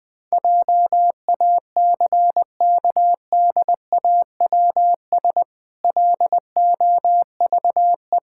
Morse code, heard as a keyed tone, about 20 wpm, 700 Hz. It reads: JACKDAWS LOVE
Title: Pangram Challenge v2 - 20wpm